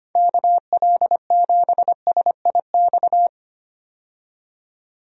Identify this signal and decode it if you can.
Morse code, heard as a keyed tone, about 25 wpm, 700 Hz. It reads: KL7SIX
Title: DXCC-100 Call Signs - 6 Characters - 25wpm